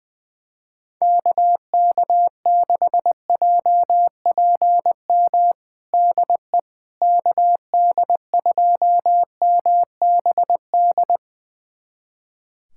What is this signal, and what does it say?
Morse code, heard as a keyed tone, about 20 wpm, 700 Hz. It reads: KK6JPM DE KD2MBD
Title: USA Callsign DE Callsign - 6 Characters - 20wpm